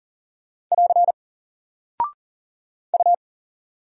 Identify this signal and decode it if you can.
Morse code, heard as a keyed tone, about 39 wpm, 700 Hz. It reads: <AR> U
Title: Single Character (Letter:Number:Punctuation:Prosign) 40wpm